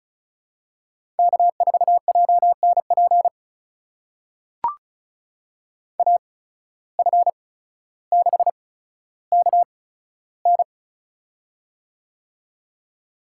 Morse code, heard as a keyed tone, about 34 wpm, spaced as for 15 wpm, 700 Hz. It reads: K4JNP AF6KN
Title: USA Call Signs - 5 Characters - 8x Character Spacing - 35wpm